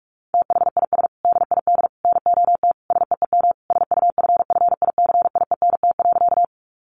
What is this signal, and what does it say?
Morse code, heard as a keyed tone, about 44 wpm, 700 Hz. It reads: THIS DID NOT SEEM SUFFICIENT.
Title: Sentences from Top 1400 Words 45wpm